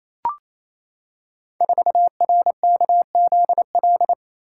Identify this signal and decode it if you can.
Morse code, heard as a keyed tone, about 29 wpm, 700 Hz. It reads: 4RKZL
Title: Code Groups - 5 Letter:Numbers - 28wpm